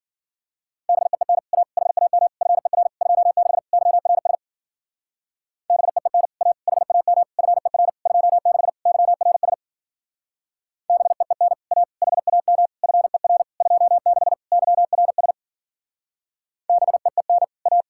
Morse code, heard as a keyed tone, about 50 wpm, 700 Hz. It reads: BEEN A HAM FER 26 YRS BEEN A HAM FER 26 YRS BEEN A HAM FER 26 YRS BEEN A